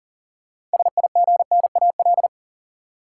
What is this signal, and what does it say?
Morse code, heard as a keyed tone, about 40 wpm, 700 Hz. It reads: SIGNAL
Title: Six Letter Words - 40wpm